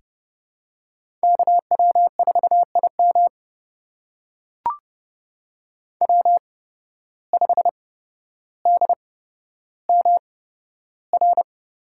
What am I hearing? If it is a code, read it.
Morse code, heard as a keyed tone, about 32 wpm, spaced as for 13 wpm, 700 Hz. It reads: KW4IM W5DMR
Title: USA Call Signs - 5 Characters - 8x Character Spacing - 30wpm